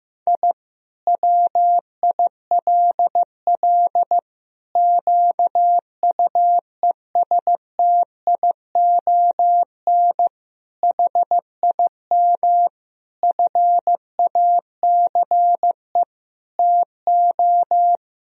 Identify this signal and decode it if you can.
Morse code, heard as a keyed tone, about 15 wpm, 700 Hz. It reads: I WILL QUESTION HIM FACE TO
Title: Sentences from Top 1000 Words - Review Part 2 - 15wpm